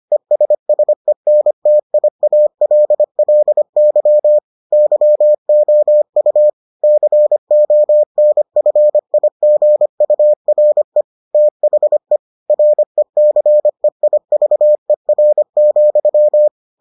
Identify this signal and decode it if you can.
Morse code, heard as a keyed tone, about 25 wpm, 600 Hz. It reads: ESSENTIALLY YOU CONFIGURE THE RECEIVER,